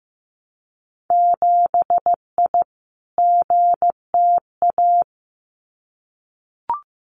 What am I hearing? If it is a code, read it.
Morse code, heard as a keyed tone, about 15 wpm, 700 Hz. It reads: 7I GTA
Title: Winter Field Day Exchanges Encore 15wpm